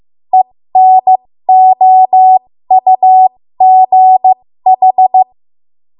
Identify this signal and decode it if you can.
Morse code, heard as a keyed tone, about 15 wpm, spaced as for 13 wpm, 750 Hz. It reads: ENOUGH
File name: Morse / 13WPM